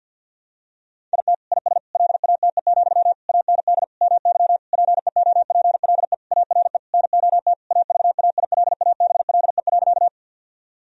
Morse code, heard as a keyed tone, about 50 wpm, 700 Hz. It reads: IT IS LATE, AND MY PEOPLE ARE NOT AVAILABLE.
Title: Sentences from Top 1000 Words - Review Part 3 - 50wpm